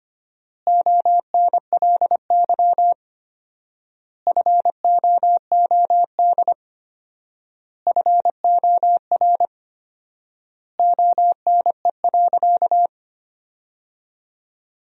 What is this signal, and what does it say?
Morse code, heard as a keyed tone, about 25 wpm, 700 Hz. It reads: ONLY FOOD FOR ONE.
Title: 4x Word-Spacing - Sentences from Top 300 Words - 25wpm